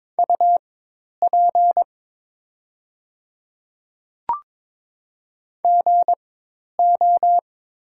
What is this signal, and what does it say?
Morse code, heard as a keyed tone, about 22 wpm, spaced as for 10 wpm, 700 Hz. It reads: UP GO
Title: 4x Character Spacing - Top 200 Words - 22wpm